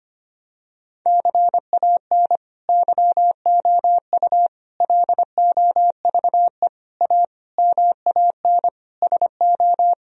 Morse code, heard as a keyed tone, about 25 wpm, 700 Hz. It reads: CAN YOU LOVE A MAN SO
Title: Sentences from Top 1000 Words - Review Part 1 - 25wpm